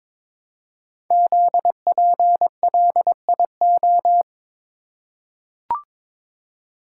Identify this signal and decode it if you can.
Morse code, heard as a keyed tone, about 22 wpm, 700 Hz. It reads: ZPLIO
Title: Code Groups - 5 Letters - 22wpm